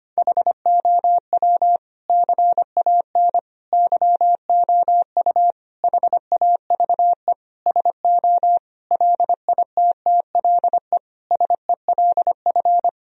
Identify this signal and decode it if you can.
Morse code, heard as a keyed tone, about 25 wpm, 700 Hz. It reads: HOW CAN YOU HAVE SO LITTLE SELF
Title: Sentences from Top 1000 Words - Review Part 1 - 25wpm